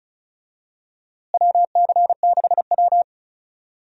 Morse code, heard as a keyed tone, about 36 wpm, 700 Hz. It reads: WC6W
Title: USA Call Signs - 4 Characters - Repeated 2x - 35wpm